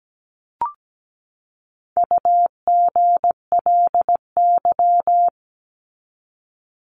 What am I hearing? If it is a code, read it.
Morse code, heard as a keyed tone, about 17 wpm, 700 Hz. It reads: UGLY
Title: Top 1751-2000 Words 17wpm